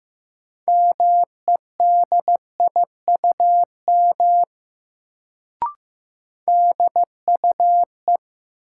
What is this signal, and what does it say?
Morse code, heard as a keyed tone, about 15 wpm, 700 Hz. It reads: MEDIUM DUE
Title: Sentences from Top 900 Words - New Word Drill - 15wpm